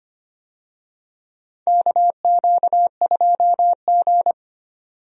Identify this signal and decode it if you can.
Morse code, heard as a keyed tone, about 25 wpm, 700 Hz. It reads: KQ2G
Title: USA Call Signs - 4 Characters - 25wpm